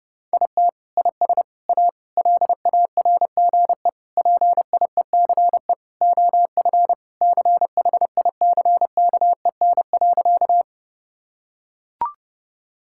Morse code, heard as a keyed tone, about 30 wpm, 700 Hz. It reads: IT IS A LARGE PIECE OF CHICKEN.
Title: Sentences from Top 1000 Words - Review Part 5 - 30wpm